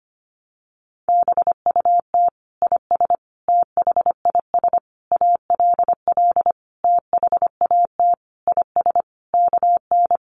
Morse code, heard as a keyed tone, about 25 wpm, 700 Hz. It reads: BUT IS THIS ALL THAT IS KN
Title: Sentences from Top 600 Words 25wpm